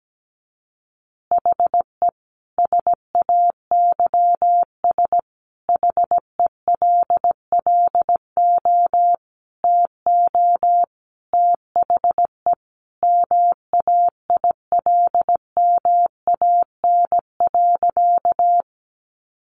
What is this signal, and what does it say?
Morse code, heard as a keyed tone, about 17 wpm, 700 Hz. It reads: HE SAYS HELLO TO THE MAILMAN.
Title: Easy Phrases 17wpm